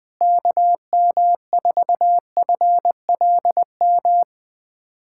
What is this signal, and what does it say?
Morse code, heard as a keyed tone, about 21 wpm, 700 Hz. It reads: KM4FLM
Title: USA Callsign DE Callsign 20wpm